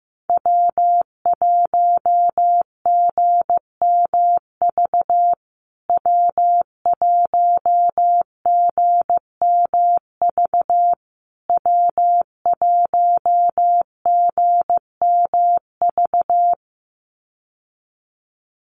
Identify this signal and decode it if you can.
Morse code, heard as a keyed tone, about 15 wpm, 700 Hz. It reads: W1GMV W1GMV W1GMV
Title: USA Call Signs Repeated 3x 15wpm